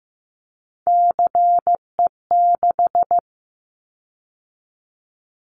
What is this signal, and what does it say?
Morse code, heard as a keyed tone, about 14 wpm, 700 Hz. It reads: CE6
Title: Callsign Prefixes for Top 100 DXCC - 15wpm